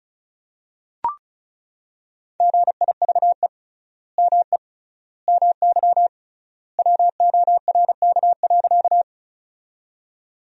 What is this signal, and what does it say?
Morse code, heard as a keyed tone, about 36 wpm, 700 Hz. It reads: GIVE ME MY WORK.
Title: 3x Word-Spacing - Sentences from Top 200 Words - 35wpm